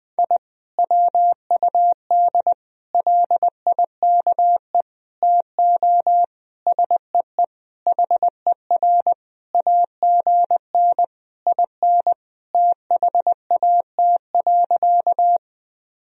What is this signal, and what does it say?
Morse code, heard as a keyed tone, about 20 wpm, 700 Hz. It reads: I WUD LIKE TO SEE HER AGN IN THAT.
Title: Sentences from Top 200 Words and Abbreviations 20wpm